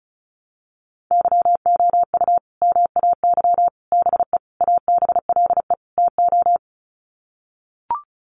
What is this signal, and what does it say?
Morse code, heard as a keyed tone, about 34 wpm, 700 Hz. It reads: YOU MAY BE ABLE TO
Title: Sets of 5 Words Encore 35wpm